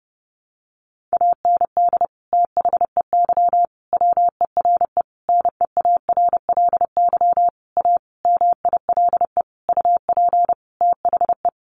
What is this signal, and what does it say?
Morse code, heard as a keyed tone, about 30 wpm, 700 Hz. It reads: AND THEY WERE NEARLY A MILE UP THE
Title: Sentences from Top 1300 Words 30wpm